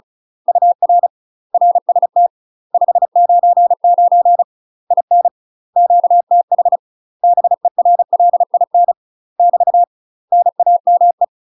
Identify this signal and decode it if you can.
Morse code, heard as a keyed tone, about 36 wpm, 700 Hz. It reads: UR RST 599 IN QTH BERLIN <BT> NAME
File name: UK MASTER QSOs TRAINING / 35WPM / UK MASTER QSOs 0016